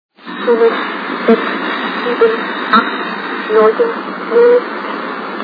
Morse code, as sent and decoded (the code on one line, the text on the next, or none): none